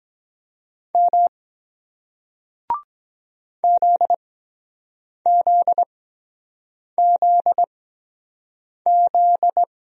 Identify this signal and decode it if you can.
Morse code, heard as a keyed tone, about 23 wpm, spaced as for 7 wpm, 700 Hz. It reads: M ZZZZ